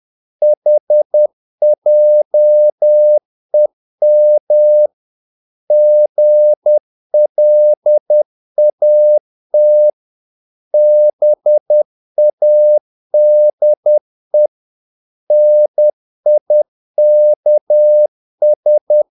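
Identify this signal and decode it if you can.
Morse code, heard as a keyed tone, about 10 wpm, 600 Hz. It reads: HJEM GLAT BADE NIKS
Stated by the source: Morse practice of short Danish words at 10 wpm, from CW med Gnister.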